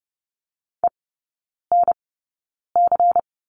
Morse code, heard as a keyed tone, about 30 wpm, spaced as for 9 wpm, 700 Hz. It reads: ENC
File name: Lesson 19e - Letter w - Three Characters 30wpm